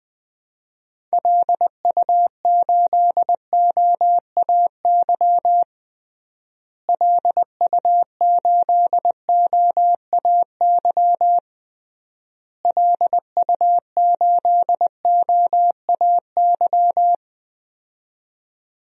Morse code, heard as a keyed tone, about 20 wpm, 700 Hz. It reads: LU8OAY LU8OAY LU8OAY
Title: DXCC-100 Call Signs - 6 Characters - Repeated 3x - 20wpm